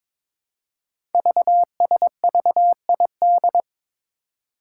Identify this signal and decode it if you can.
Morse code, heard as a keyed tone, about 22 wpm, 700 Hz. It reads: VSVID
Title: Code Groups - 5 Letters - 22wpm